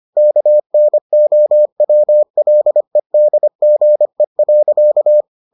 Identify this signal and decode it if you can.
Morse code, heard as a keyed tone, about 25 wpm, 600 Hz. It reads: KNOWLEDGE.